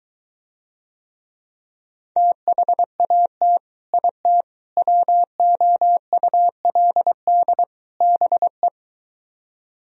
Morse code, heard as a keyed tone, about 23 wpm, 700 Hz. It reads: THAT IT WOULD BE